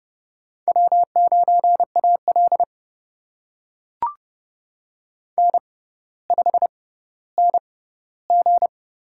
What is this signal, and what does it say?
Morse code, heard as a keyed tone, about 30 wpm, spaced as for 15 wpm, 700 Hz. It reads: W9AL N5NG
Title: USA Call Signs - 4 Characters - 6x Character Spacing - 30wpm